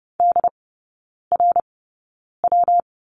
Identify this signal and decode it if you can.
Morse code, heard as a keyed tone, about 30 wpm, spaced as for 9 wpm, 700 Hz. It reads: DRW
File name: Lesson 19e - Letter w - Three Characters 30wpm